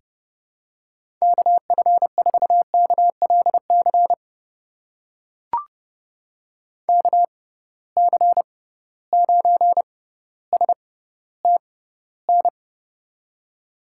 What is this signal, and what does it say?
Morse code, heard as a keyed tone, about 30 wpm, spaced as for 15 wpm, 700 Hz. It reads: KF4KLC KC9STN
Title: USA Call Signs - 6 Characters - 6x Character Spacing - 30wpm